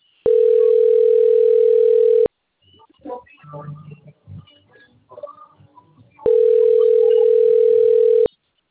Ringback tone (North America), 440 + 480 Hz.
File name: rg-1600-unknown- - - .21828